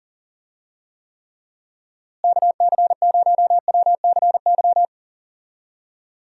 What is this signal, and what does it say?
Morse code, heard as a keyed tone, about 39 wpm, 700 Hz. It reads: KC0WCY
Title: USA Call Signs - 6 Characters - Repeated 3x - 40wpm